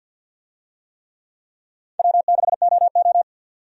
Morse code, heard as a keyed tone, about 51 wpm, 700 Hz. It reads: W6OO
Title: USA Call Signs - 4 Characters - 50wpm